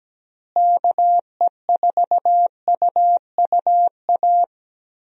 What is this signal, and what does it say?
Morse code, heard as a keyed tone, about 17 wpm, 700 Hz. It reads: KE4UUA
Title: USA Call Signs - 6 Characters - Repeated 2x - 17wpm